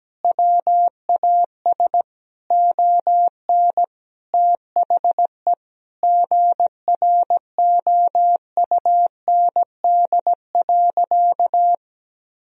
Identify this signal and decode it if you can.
Morse code, heard as a keyed tone, about 17 wpm, 700 Hz. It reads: WAS ON THE GROUND.
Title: Sentences from Top 1000 Words - Review Part 2 - 17wpm